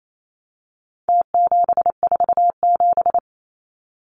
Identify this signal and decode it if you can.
Morse code, heard as a keyed tone, about 28 wpm, 700 Hz. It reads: T747
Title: Numbers - Contest Style - 4-Digit 28wpm